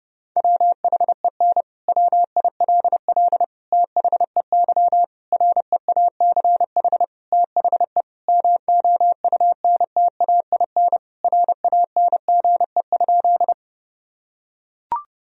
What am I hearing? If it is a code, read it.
Morse code, heard as a keyed tone, about 32 wpm, 700 Hz. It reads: WHEN WILL THEY REACH THE MOUNTAIN RANGE?
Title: Sentences from Top 1000 Words - Review Part 4 - 30wpm